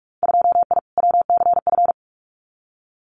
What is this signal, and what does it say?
Morse code, heard as a keyed tone, about 46 wpm, 700 Hz. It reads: 2I WCF
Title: Winter Field Day Exchanges 45wpm